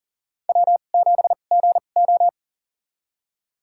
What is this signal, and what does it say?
Morse code, heard as a keyed tone, about 40 wpm, spaced as for 29 wpm, 700 Hz. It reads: W7GO
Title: USA Call Signs - 4 Characters - 2x Character Spacing - 40wpm